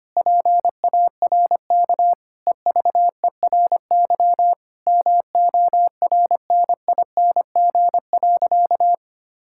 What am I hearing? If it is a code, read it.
Morse code, heard as a keyed tone, about 25 wpm, 700 Hz. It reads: PARK EVERY MORNING.
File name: practice / 25wpm / Sentences with 2-Word Phrases - 25wpm